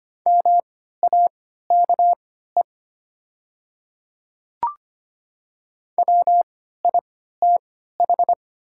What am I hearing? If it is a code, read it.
Morse code, heard as a keyed tone, about 25 wpm, spaced as for 14 wpm, 700 Hz. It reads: MAKE WITH
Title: 3x Character Spacing - Top 100 Words - 25wpm